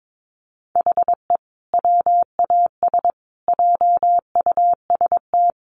Morse code, heard as a keyed tone, about 22 wpm, 700 Hz. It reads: HE WAS JUST
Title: Sentences from Top 1100 Words 22wpm